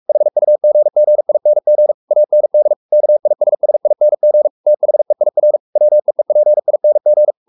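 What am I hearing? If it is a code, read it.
Morse code, heard as a keyed tone, about 44 wpm, 600 Hz. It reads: HUGGING AND KISSING THEIR WEEPING
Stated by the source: Morse code at 43 wpm sending Ken Follett's novel Never in English, from CW med Gnister.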